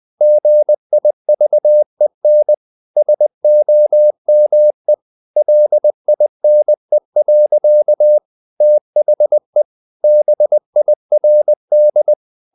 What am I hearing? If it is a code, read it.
Morse code, heard as a keyed tone, about 20 wpm, 600 Hz. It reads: GIVEN SOME LINE. THE BIRD
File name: the old man and the sea / the old man and the sea part 1 20wpm